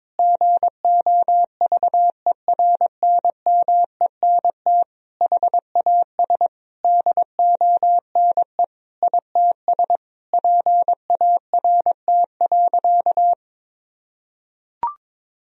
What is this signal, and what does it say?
Morse code, heard as a keyed tone, about 22 wpm, 700 Hz. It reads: GOVERNMENT HAS DONE ITS PART.
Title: Sentences from Top 1000 Words - Review Part 3 - 22wpm